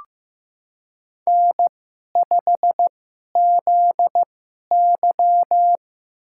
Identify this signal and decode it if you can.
Morse code, heard as a keyed tone, about 15 wpm, spaced as for 11 wpm, 700 Hz. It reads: N5ZY
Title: USA Call Signs - 4 Characters - 2x Character Spacing - 15wpm